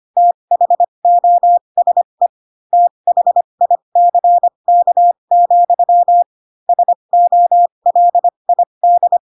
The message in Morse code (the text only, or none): THOSE THICK, SOLID